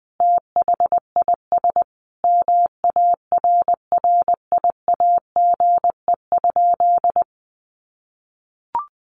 Morse code, heard as a keyed tone, about 20 wpm, 700 Hz. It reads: THIS MARRIAGE?
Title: Sentences from Top 1400 Words 20wpm